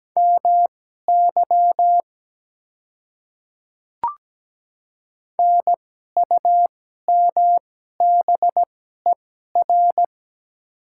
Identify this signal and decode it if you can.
Morse code, heard as a keyed tone, about 17 wpm, spaced as for 12 wpm, 700 Hz. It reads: MY NUMBER
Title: 2x Character Spacing - Top 100 Words - 17wpm